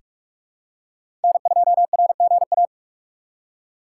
Morse code, heard as a keyed tone, about 45 wpm, 700 Hz. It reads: N2RGA
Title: USA Call Signs - 5 Characters - 2x Character Spacing - 45wpm